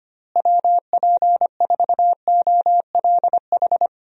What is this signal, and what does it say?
Morse code, heard as a keyed tone, about 25 wpm, 700 Hz. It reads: WP4OLH
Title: USA Call Signs - 6 Characters - Repeated 3x - 25wpm